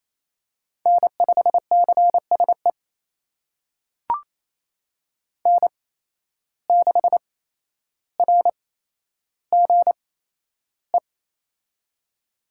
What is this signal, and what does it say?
Morse code, heard as a keyed tone, about 28 wpm, spaced as for 12 wpm, 700 Hz. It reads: N5CSE N6RGE
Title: USA Call Signs - 5 Characters - 8x Character Spacing - 28wpm